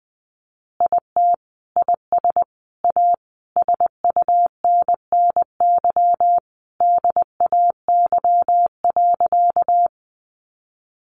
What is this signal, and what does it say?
Morse code, heard as a keyed tone, about 20 wpm, 700 Hz. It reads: IT IS A SUNNY DAY.
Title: Easy Phrases 20wpm